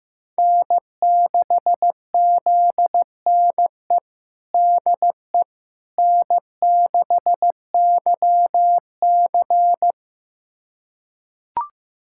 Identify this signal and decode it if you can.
Morse code, heard as a keyed tone, about 15 wpm, 700 Hz. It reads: N6ZNE DE N6YC
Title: USA Callsign DE Callsign 15wpm